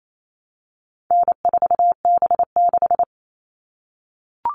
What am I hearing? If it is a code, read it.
Morse code, heard as a keyed tone, about 28 wpm, 700 Hz. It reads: N4B6
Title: All Cut Numbers - 4-Digit 28wpm